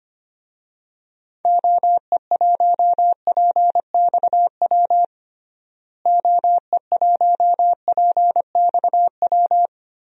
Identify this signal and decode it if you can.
Morse code, heard as a keyed tone, about 25 wpm, 700 Hz. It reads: OE1PXW OE1PXW
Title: DXCC-100 Call Signs - 6 Characters - Repeated 2x - 25wpm